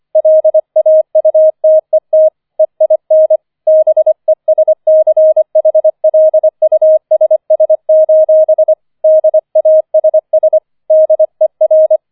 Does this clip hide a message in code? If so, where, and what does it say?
Yes: Morse code from 0.2 to 12.0 s, heard as a keyed tone, about 25 wpm, 630 Hz. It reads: LAUTET EIN BESCHLUSS: DASS DER